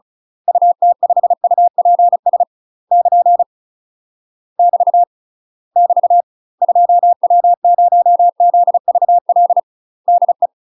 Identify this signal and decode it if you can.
Morse code, heard as a keyed tone, about 36 wpm, 700 Hz. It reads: UT5UPS <KN> <BT> <BT> 2W0ZVL DE
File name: UK MASTER QSOs TRAINING / 35WPM / UK MASTER QSOs 0023